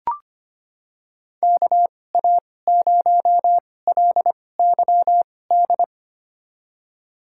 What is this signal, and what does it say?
Morse code, heard as a keyed tone, about 25 wpm, spaced as for 18 wpm, 700 Hz. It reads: KA0LYD
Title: USA Call Signs - 6 Characters - 2x Character Spacing - 25wpm